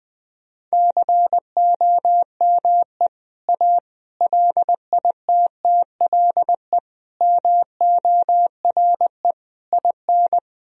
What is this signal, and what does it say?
Morse code, heard as a keyed tone, about 21 wpm, 700 Hz. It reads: COME A LITTLE MORE IN
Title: Sentences from Top 1000 Words - Review Part 2 - 20wpm